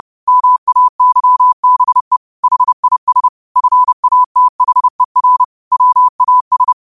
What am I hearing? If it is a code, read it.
Morse code, heard as a keyed tone, about 30 wpm, 1000 Hz. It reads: MAYBE HIS FATHER WAS
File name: qotdfiles / 30wpm